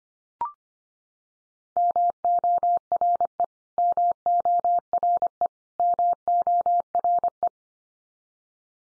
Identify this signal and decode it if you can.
Morse code, heard as a keyed tone, about 25 wpm, 700 Hz. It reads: MORE MORE MORE
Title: Top 100 Words 25wpm x3